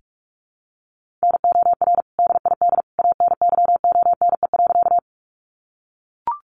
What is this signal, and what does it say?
Morse code, heard as a keyed tone, about 44 wpm, 700 Hz. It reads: NOR DID ANYONE.
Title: Sentences from Top 1500 Words 45wpm